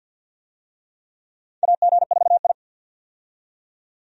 Morse code, heard as a keyed tone, about 53 wpm, 700 Hz. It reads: AG4I
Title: USA Call Signs - 4 Characters - 50wpm